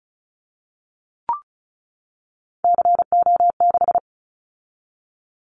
Morse code, heard as a keyed tone, about 36 wpm, 700 Hz. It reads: CO6
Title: Callsign Prefixes for Top 100 DXCC - 35wpm